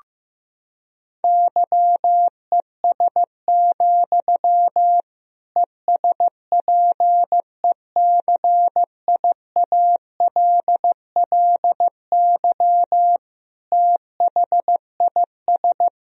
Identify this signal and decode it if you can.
Morse code, heard as a keyed tone, about 15 wpm, 700 Hz. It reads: YES, ESPECIALLY THIS
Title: Sentences from Top 1000 Words - Review Part 4 - 15wpm